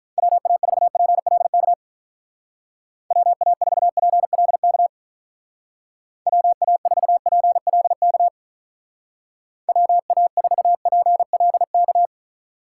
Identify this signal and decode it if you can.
Morse code, heard as a keyed tone, about 44 wpm, 700 Hz. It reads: WA4PLK WA4PLK WA4PLK WA4PLK